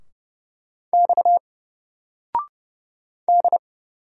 Morse code, heard as a keyed tone, about 30 wpm, 700 Hz. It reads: X D
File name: Single-Letter-Number-30wpm